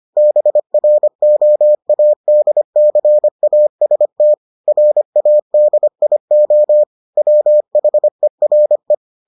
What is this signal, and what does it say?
Morse code, heard as a keyed tone, about 25 wpm, 600 Hz. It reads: BROADCAST RADIO WHERE